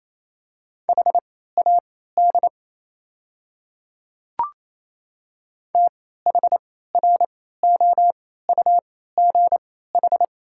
Morse code, heard as a keyed tone, about 28 wpm, spaced as for 16 wpm, 700 Hz. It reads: HAD THROUGH
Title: 3x Character Spacing - Top 200 Words - 28wpm